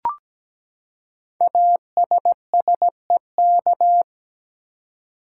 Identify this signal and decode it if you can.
Morse code, heard as a keyed tone, about 17 wpm, 700 Hz. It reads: ASSEK
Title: Code Groups - 5 Letters - 17wpm